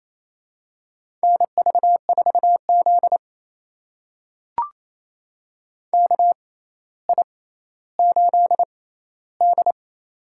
Morse code, heard as a keyed tone, about 28 wpm, spaced as for 14 wpm, 700 Hz. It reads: NV4Z KI8D
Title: USA Call Signs - 6x Character Spacing - 28wpm